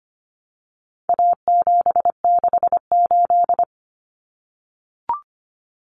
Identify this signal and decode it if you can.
Morse code, heard as a keyed tone, about 25 wpm, 700 Hz. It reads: A768
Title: Common Cut Numbers - 4-Digit 25wpm